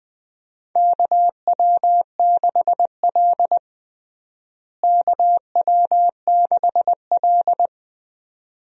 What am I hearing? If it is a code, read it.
Morse code, heard as a keyed tone, about 20 wpm, 700 Hz. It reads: KW6L KW6L
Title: USA Call Signs - 4 Characters - Repeated 2x - 20wpm